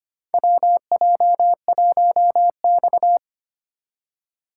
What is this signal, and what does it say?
Morse code, heard as a keyed tone, about 25 wpm, 700 Hz. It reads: WJ1X
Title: USA Call Signs - 2x Character Spacing - 25wpm